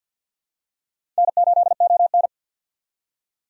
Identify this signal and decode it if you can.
Morse code, heard as a keyed tone, about 47 wpm, 700 Hz. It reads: N8ON
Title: USA Call Signs - 4 Characters - 50wpm